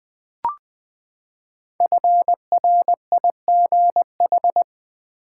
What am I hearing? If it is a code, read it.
Morse code, heard as a keyed tone, about 20 wpm, 700 Hz. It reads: FRIGH
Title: Build A Word - 5 Letters At A Time - 10-Letter Words - 20wpm